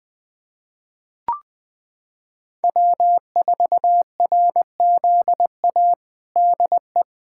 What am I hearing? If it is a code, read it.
Morse code, heard as a keyed tone, about 20 wpm, 700 Hz. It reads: W4RZA DE
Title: Practice QSO - Exchange 4 - Ending - 20wpm